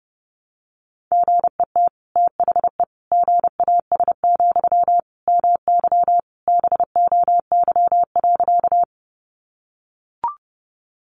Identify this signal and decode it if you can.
Morse code, heard as a keyed tone, about 30 wpm, 700 Hz. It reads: GET THE GAS, MY BOY.
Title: Sentences from Top 1100 Words 30wpm